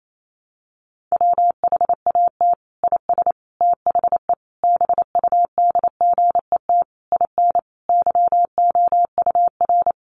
Morse code, heard as a keyed tone, about 28 wpm, 700 Hz. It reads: WHAT IS THE BUDGET IN YOUR
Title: Sentences from Top 1200 Words 28wpm